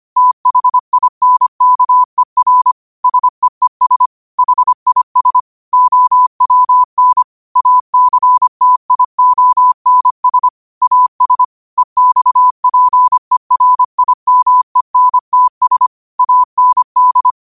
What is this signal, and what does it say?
Morse code, heard as a keyed tone, about 25 wpm, 1000 Hz. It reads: THINKER SEES HIS OWN ACTIONS AS EXPERIMENTS AND
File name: qotdfiles / 25wpm